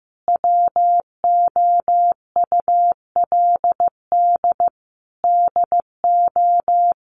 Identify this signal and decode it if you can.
Morse code, heard as a keyed tone, about 15 wpm, 700 Hz. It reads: WOULD DO
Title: Sentences from Top 1500 Words 15wpm